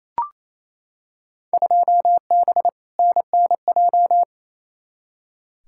Morse code, heard as a keyed tone, about 28 wpm, 700 Hz. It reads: 2B NNJ
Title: Field Day Exchanges Encore - 28wpm